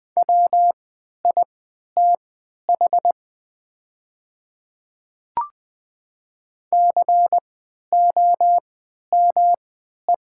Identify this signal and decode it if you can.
Morse code, heard as a keyed tone, about 20 wpm, spaced as for 11 wpm, 700 Hz. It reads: WITH COME
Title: 3x Character Spacing - Top 100 Words - 20wpm